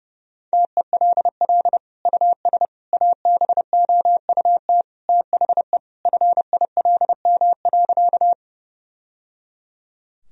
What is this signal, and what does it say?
Morse code, heard as a keyed tone, about 29 wpm, 700 Hz. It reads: TELL US ABOUT THE FILM.
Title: Sentences from Top 1000 Words - Review Part 3 - 30wpm